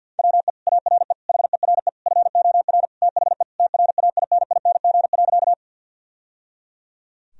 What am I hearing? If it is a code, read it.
Morse code, heard as a keyed tone, about 48 wpm, 700 Hz. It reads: WE ARE HERE FOR THE TRAINING.
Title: Sentences from Top 1000 Words - Review Part 4 - 50wpm